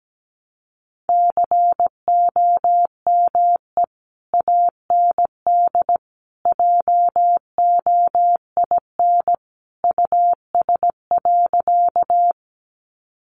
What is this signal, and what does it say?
Morse code, heard as a keyed tone, about 17 wpm, 700 Hz. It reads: COME AND JOIN US.
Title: Sentences from Top 1400 Words 17wpm